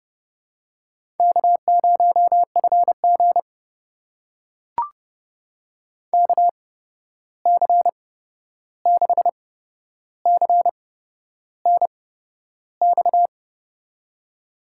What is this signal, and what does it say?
Morse code, heard as a keyed tone, about 32 wpm, spaced as for 11 wpm, 700 Hz. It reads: K0FG KC6CNX
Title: USA Call Signs - 8x Character Spacing - 30wpm